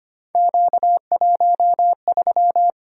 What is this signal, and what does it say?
Morse code, heard as a keyed tone, about 25 wpm, 700 Hz. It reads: Q13
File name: Three Letter-Number Combinations - 25wpm